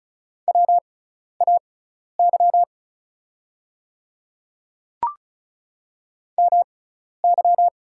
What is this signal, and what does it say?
Morse code, heard as a keyed tone, about 34 wpm, spaced as for 12 wpm, 690 Hz. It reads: WAY MY
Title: 6x Character Spacing - Top 100 Words - 35wpm